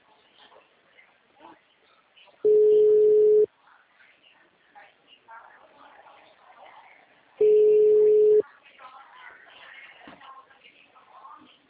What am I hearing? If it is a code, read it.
Ringback tone (Europe, 425 Hz), with speech.